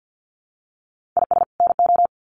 Morse code, heard as a keyed tone, about 50 wpm, 700 Hz. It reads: IS NO
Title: Sets of 2 Words 50wpm